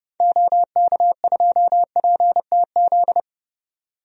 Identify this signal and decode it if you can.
Morse code, heard as a keyed tone, about 30 wpm, 700 Hz. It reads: OK2PTZ
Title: DXCC-100 Call Signs - 6 Characters - 30wpm